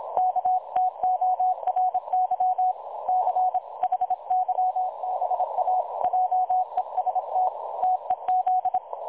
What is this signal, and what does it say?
Morse code, heard as a keyed tone, about 26 wpm, 740 Hz. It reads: KTORY CHYBA JEST TEZ
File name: dlazaawansowanych / nagraniazpasma / websdr recording 3565.0kHz